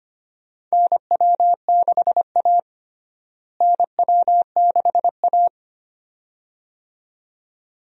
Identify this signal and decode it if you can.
Morse code, heard as a keyed tone, about 25 wpm, 700 Hz. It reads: NW6A NW6A
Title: USA Call Signs - 4 Characters - Repeated 2x - 25wpm